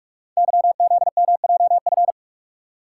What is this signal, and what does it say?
Morse code, heard as a keyed tone, about 44 wpm, 700 Hz. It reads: YZMJF
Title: Code Groups - 5 Letters - 45wpm